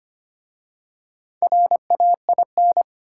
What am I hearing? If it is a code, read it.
Morse code, heard as a keyed tone, about 25 wpm, 700 Hz. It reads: RAIN
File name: practice / 25wpm / Easy Words with Hints - 25wpm